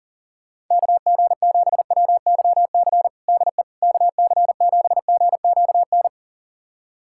Morse code, heard as a keyed tone, about 40 wpm, 700 Hz. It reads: KG7WYC DE KC7GQN
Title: USA Callsign DE Callsign 40wpm